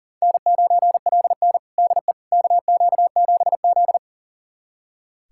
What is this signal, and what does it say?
Morse code, heard as a keyed tone, about 40 wpm, 700 Hz. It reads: N9LN DE KQ7Z
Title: USA Callsign DE Callsign - 4 Characters - 40wpm